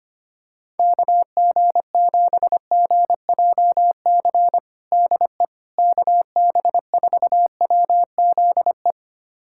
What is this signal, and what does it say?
Morse code, heard as a keyed tone, about 25 wpm, 700 Hz. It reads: KG7GJC DE KB4WZE
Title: USA Callsign DE Callsign 25wpm